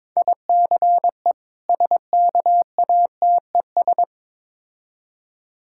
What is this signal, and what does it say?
Morse code, heard as a keyed tone, about 22 wpm, 690 Hz. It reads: ICE SKATES
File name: Difficult Words with Hints - 22wpm